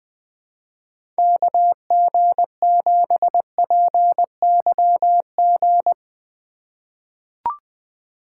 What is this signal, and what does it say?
Morse code, heard as a keyed tone, about 20 wpm, 700 Hz. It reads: KG7PYG
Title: USA Call Signs - 3x Character Spacing - 20wpm